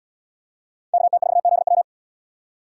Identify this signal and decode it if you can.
Morse code, heard as a keyed tone, about 77 wpm, 700 Hz. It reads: KT4ZW